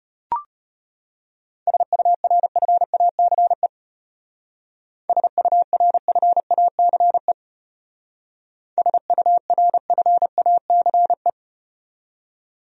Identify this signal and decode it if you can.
Morse code, heard as a keyed tone, about 33 wpm, 700 Hz. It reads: SURFACE SURFACE SURFACE